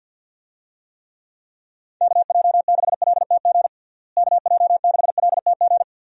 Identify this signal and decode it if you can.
Morse code, heard as a keyed tone, about 50 wpm, 700 Hz. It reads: KJ6LTG KJ6LTG
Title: USA Call Signs - 6 Characters - Repeated 2x - 50wpm